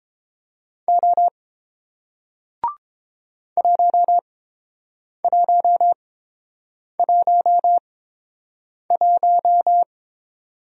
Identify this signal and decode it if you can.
Morse code, heard as a keyed tone, about 29 wpm, spaced as for 7 wpm, 700 Hz. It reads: O 1111